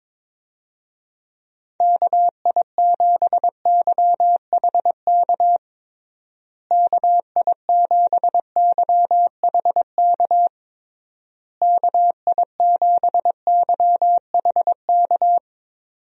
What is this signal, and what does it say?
Morse code, heard as a keyed tone, about 22 wpm, 700 Hz. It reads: KI7YHK KI7YHK KI7YHK
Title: USA Call Signs - 6 Characters - Repeated 3x - 22wpm